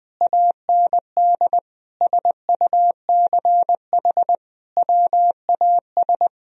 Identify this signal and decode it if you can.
Morse code, heard as a keyed tone, about 20 wpm, 700 Hz. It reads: AND SUCH WAS
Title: Sentences from Top 1000 Words - Review Part 4 - 20wpm